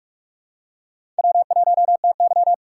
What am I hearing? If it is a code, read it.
Morse code, heard as a keyed tone, about 47 wpm, 700 Hz. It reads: W1TY
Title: USA Call Signs - 4 Characters - 6x Character Spacing - 45wpm